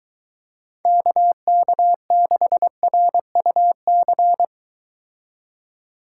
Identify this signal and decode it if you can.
Morse code, heard as a keyed tone, about 23 wpm, 700 Hz. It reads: KK6RUC